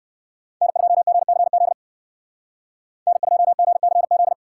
Morse code, heard as a keyed tone, about 58 wpm, 700 Hz. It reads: N2GCZ N2GCZ